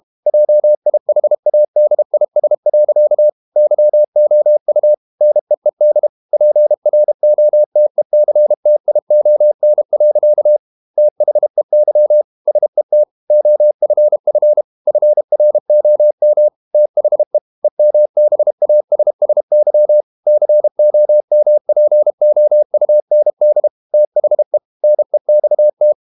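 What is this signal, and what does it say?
Morse code, heard as a keyed tone, about 32 wpm, 600 Hz. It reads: JIHADIS. YOU NEED PROTECTION. THEY SET OFF FROM THE EMBASSY COMPOUND THE NEXT
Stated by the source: Morse code practice at 32 wpm from CW med Gnister, reading Ken Follett's novel Never in English.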